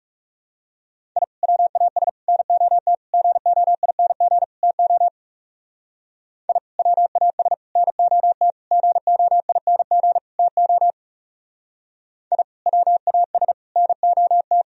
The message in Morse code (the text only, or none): I WAS NOT GOING TO I WAS NOT GOING TO I WAS NOT